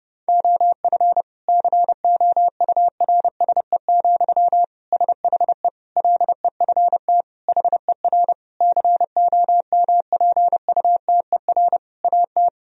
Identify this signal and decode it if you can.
Morse code, heard as a keyed tone, about 30 wpm, 700 Hz. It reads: OF COURSE, SHE LEFT HER COMPUTER AT
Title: Sentences from Top 1000 Words - Review Part 3 - 30wpm